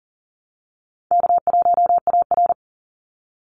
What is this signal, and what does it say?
Morse code, heard as a keyed tone, about 40 wpm, 700 Hz. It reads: K1AR
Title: Easy Call Signs Repeated 2x - 40wpm